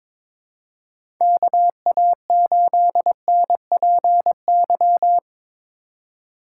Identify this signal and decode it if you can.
Morse code, heard as a keyed tone, about 22 wpm, 700 Hz. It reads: KA8NPY
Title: USA Call Signs - 6 Characters - 4x Character Spacing - 22wpm